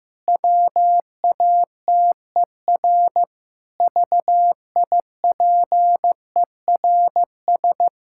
Morse code, heard as a keyed tone, about 15 wpm, 700 Hz. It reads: WATER VIPERS
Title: Pangram Challenge v2 - 15wpm